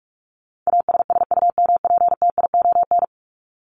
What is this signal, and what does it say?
Morse code, heard as a keyed tone, about 44 wpm, 700 Hz. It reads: ASSUMPTION
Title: Top 1751-2000 Words 45wpm